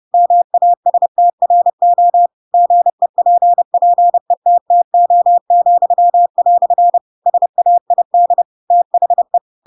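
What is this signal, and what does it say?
Morse code, heard as a keyed tone, about 30 wpm, 700 Hz. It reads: MASTRO GEPPETTO," SAID THE
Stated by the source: Chapter 2 of The Adventures of Pinocchio read as a book in Morse code (CW) at 30 wpm, by ik1hge.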